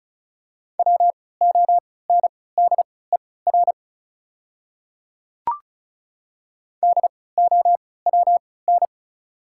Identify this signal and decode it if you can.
Morse code, heard as a keyed tone, about 36 wpm, spaced as for 20 wpm, 690 Hz. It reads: WONDER DOWN
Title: 3x Character Spacing - Top 500 Words - 35wpm